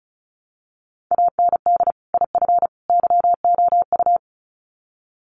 Morse code, heard as a keyed tone, about 34 wpm, 700 Hz. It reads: AND IF YOU
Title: Sets of 3 Words 35wpm